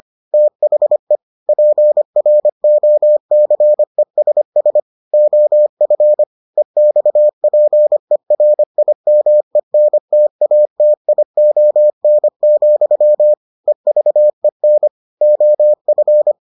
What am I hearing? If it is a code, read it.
Morse code, heard as a keyed tone, about 25 wpm, 600 Hz. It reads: THE PROCESS OF EXPERIMENTATION, EVEN OF